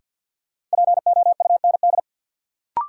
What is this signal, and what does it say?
Morse code, heard as a keyed tone, about 53 wpm, 700 Hz. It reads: POUND
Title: Sentences from Top 1000 Words - Review Part 2 - New Word Drill - 50wpm